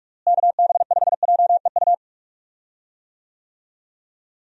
Morse code, heard as a keyed tone, about 42 wpm, 700 Hz. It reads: KB5JEU
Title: USA Call Signs - 6 Characters - Repeated 2x - 45wpm